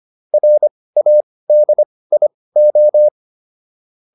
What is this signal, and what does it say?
Morse code, heard as a keyed tone, about 25 wpm, spaced as for 18 wpm, 600 Hz. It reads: RADIO